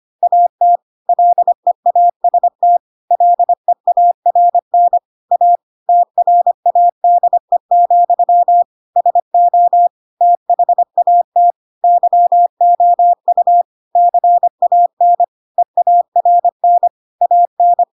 Morse code, heard as a keyed tone, about 25 wpm, 700 Hz. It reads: AT LEAST LEARN A TRADE, SO THAT YOU CAN EARN AN
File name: Pinocchio / Pinocchio ch4 25 25 ik1hge